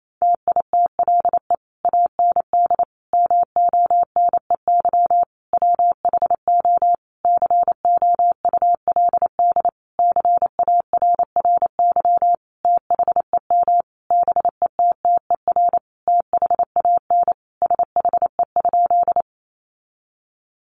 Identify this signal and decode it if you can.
Morse code, heard as a keyed tone, about 28 wpm, 700 Hz. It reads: TITLE AND MONEY WHO COULD CARRY THEM BETTER THAN SHE?
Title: Sentences from Top 1200 Words 28wpm